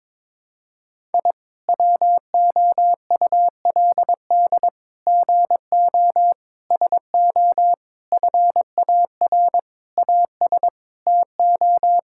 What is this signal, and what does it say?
Morse code, heard as a keyed tone, about 22 wpm, 700 Hz. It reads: I WOULD GO SO FAR AS TO
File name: Sets of 9 Words - 22wpm